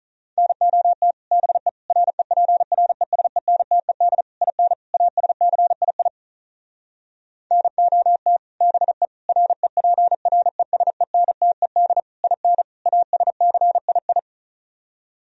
Morse code, heard as a keyed tone, about 44 wpm, 700 Hz. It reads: NOT BE REPRESENTED IN ASCII NOT BE REPRESENTED IN ASCII